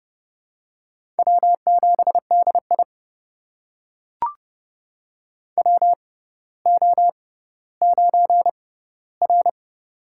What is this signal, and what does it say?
Morse code, heard as a keyed tone, about 30 wpm, spaced as for 15 wpm, 700 Hz. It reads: W7DI WO9R
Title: USA Call Signs - 4 Characters - 6x Character Spacing - 30wpm